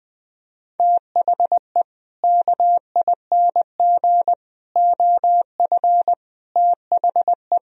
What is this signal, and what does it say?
Morse code, heard as a keyed tone, about 21 wpm, 700 Hz. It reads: THE KING OF THE
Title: Sentences from Top 1000 Words - Review Part 2 - 20wpm